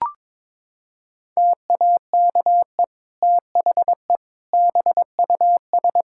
Morse code, heard as a keyed tone, about 22 wpm, 700 Hz. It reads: TAKE THE BUS
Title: Sentences from Top 1000 Words - Review Part 6 - 22wpm